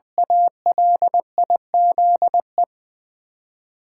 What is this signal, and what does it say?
Morse code, heard as a keyed tone, about 20 wpm, 700 Hz. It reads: ALIZE
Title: Ten Letter Words - 20wpm